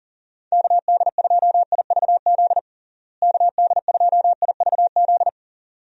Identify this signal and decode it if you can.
Morse code, heard as a keyed tone, about 40 wpm, 700 Hz. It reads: KD2IVZ KD2IVZ
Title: USA Call Signs - 6 Characters - Repeated 2x - 40wpm